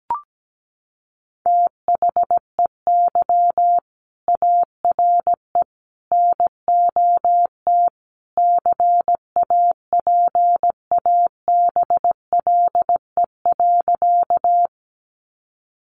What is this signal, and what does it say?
Morse code, heard as a keyed tone, about 17 wpm, 700 Hz. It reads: THEY ARE NOT CAPABLE.
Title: Sentences from Top 1200 Words 17wpm